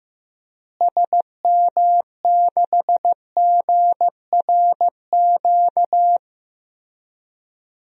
Morse code, heard as a keyed tone, about 15 wpm, 700 Hz. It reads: SM6GRQ
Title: DXCC-100 Call Signs - 6 Characters - Repeated 2x - 15wpm